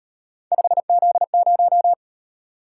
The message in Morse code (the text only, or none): 5Z0